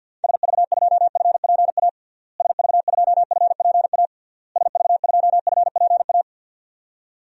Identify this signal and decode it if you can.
Morse code, heard as a keyed tone, about 50 wpm, 700 Hz. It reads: SV2FPA SV2FPA SV2FPA
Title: DXCC-100 Call Signs - 6 Characters - Repeated 3x - 50wpm